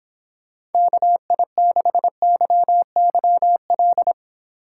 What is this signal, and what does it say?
Morse code, heard as a keyed tone, about 29 wpm, 700 Hz. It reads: KI6YYL